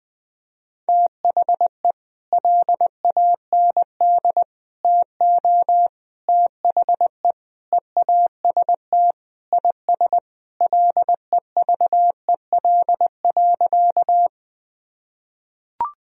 Morse code, heard as a keyed tone, about 21 wpm, 700 Hz. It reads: THE LAND TO THE EAST IS LEVEL.
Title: Sentences from Top 1000 Words - Review Part 4 - 20wpm